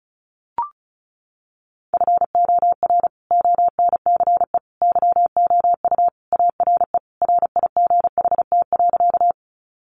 Morse code, heard as a keyed tone, about 34 wpm, 700 Hz. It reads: FOR ONCE YOU ARE RIGHT.
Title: Sentences from Top 300 Words 35wpm